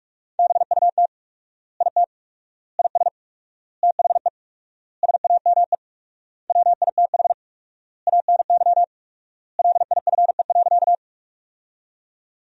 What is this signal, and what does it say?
Morse code, heard as a keyed tone, about 41 wpm, 700 Hz. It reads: BUT IT IS THE SAME WITH ANY LIFE.
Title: 4x Word-Spacing - Sentences from Top 300 Words - 45wpm